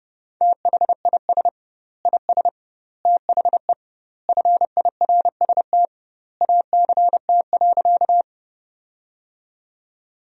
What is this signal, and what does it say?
Morse code, heard as a keyed tone, about 30 wpm, 700 Hz. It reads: THIS IS THE FIRST ACT.
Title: 2x Word-Spacing - Sentences from Top 200 Words - 30wpm